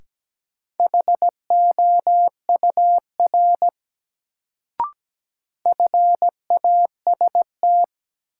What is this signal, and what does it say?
Morse code, heard as a keyed tone, about 17 wpm, 700 Hz. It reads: HOUR FAST
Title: Sentences from Top 400 Words - New Word Drill - 17wpm